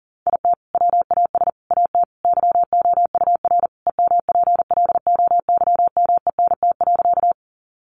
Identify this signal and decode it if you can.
Morse code, heard as a keyed tone, about 40 wpm, 700 Hz. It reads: IT WAS AT YOUR EMPLOYMENT.
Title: Sentences from Top 1500 Words 40wpm